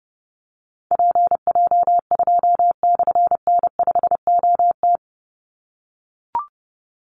Morse code, heard as a keyed tone, about 30 wpm, 700 Hz. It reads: PJ2/N5OT
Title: Call Signs - 30wpm